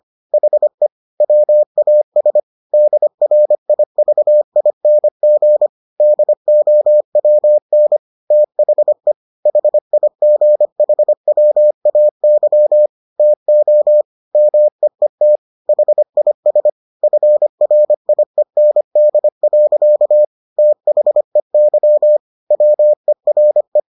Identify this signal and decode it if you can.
Morse code, heard as a keyed tone, about 25 wpm, 600 Hz. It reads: HE WAS DRIVING DOWN THE HIGHWAY TO MEET HIS FRIEND. THEY WERE